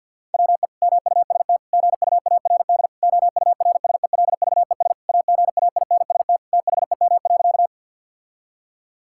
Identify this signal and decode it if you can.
Morse code, heard as a keyed tone, about 46 wpm, 700 Hz. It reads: WE MUST GUARD OURSELVES AGAINST THEM.
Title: Sentences from Top 1600 Words 50wpm